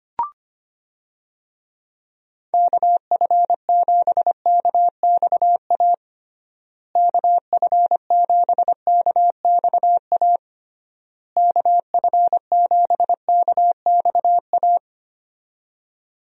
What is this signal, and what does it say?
Morse code, heard as a keyed tone, about 25 wpm, 700 Hz. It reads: KF7KXA KF7KXA KF7KXA
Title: USA Call Signs - 6 Characters - Repeated 3x - 25wpm